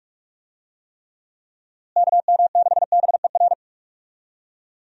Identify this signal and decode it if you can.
Morse code, heard as a keyed tone, about 41 wpm, 700 Hz. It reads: KM6BER
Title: USA Call Signs - 6 Characters - Repeated 3x - 45wpm